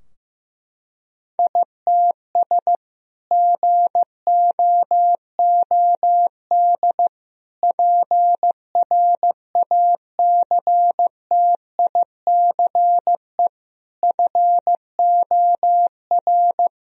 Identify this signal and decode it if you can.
Morse code, heard as a keyed tone, about 16 wpm, 700 Hz. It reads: ITS GOOD PRACTICE FOR
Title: Sentences from Top 1000 Words - Review Part 4 - 15wpm